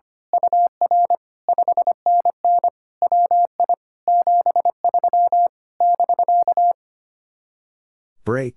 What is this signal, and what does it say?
Morse code, heard as a keyed tone, about 24 wpm, 700 Hz. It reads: UR 5NN WI 73 <BK>
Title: Speed-Racing Common QSO Phrases - 25wpm